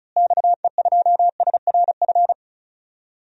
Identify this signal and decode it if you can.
Morse code, heard as a keyed tone, about 34 wpm, 700 Hz. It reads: XE2SRF
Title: DXCC-100 Call Signs - 6 Characters - Repeated 3x - 35wpm